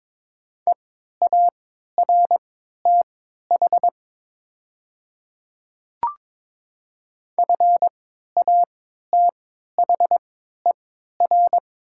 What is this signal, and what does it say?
Morse code, heard as a keyed tone, about 22 wpm, spaced as for 12 wpm, 700 Hz. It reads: EARTH FATHER
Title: 3x Character Spacing - Top 200 Words - 22wpm